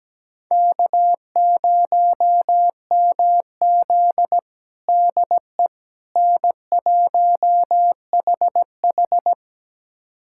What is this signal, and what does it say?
Morse code, heard as a keyed tone, about 17 wpm, 700 Hz. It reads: K0MZ DE N1HH
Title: USA Callsign DE Callsign - 4 Characters - 17wpm